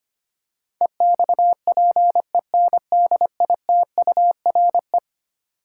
Morse code, heard as a keyed tone, about 25 wpm, 700 Hz. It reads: EXPENDITURE
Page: morse-code-ninja-practice-25wpm